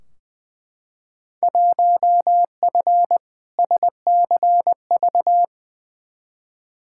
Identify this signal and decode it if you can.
Morse code, heard as a keyed tone, about 21 wpm, 700 Hz. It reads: 1F SCV
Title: Field Day Exchanges Encore - 20wpm